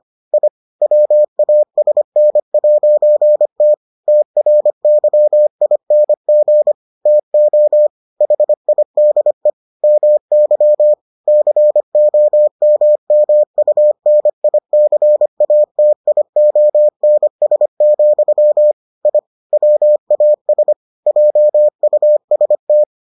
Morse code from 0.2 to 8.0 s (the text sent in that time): I WASN'T TRYING TO